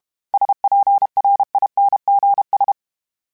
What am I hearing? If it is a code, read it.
Morse code, heard as a keyed tone, about 32 wpm, 800 Hz. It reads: SPRINGS
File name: fromSMs / FromSMs mp3s / Cities@30